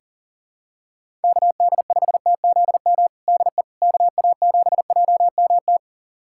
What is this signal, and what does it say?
Morse code, heard as a keyed tone, about 37 wpm, 700 Hz. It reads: KD5TZM DE KA7JMT
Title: USA Callsign DE Callsign 40wpm